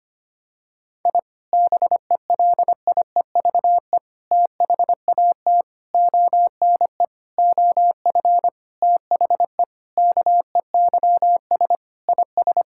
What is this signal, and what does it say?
Morse code, heard as a keyed tone, about 25 wpm, 700 Hz. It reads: I BELIEVE THAT ONE OF THE KEYS IS
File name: practice / 25wpm / Sentences with 4-Word Phrases - 25wpm